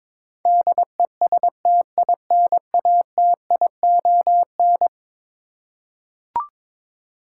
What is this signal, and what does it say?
Morse code, heard as a keyed tone, about 22 wpm, 700 Hz. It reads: DESTINATION
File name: practice / 22wpm / Eleven Letter Words - 22wpm